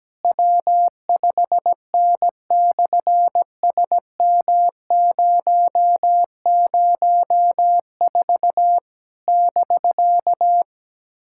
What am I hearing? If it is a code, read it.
Morse code, heard as a keyed tone, about 17 wpm, 700 Hz. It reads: W5N/SM004 <BK>
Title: SOTA QSO Practice 17wpm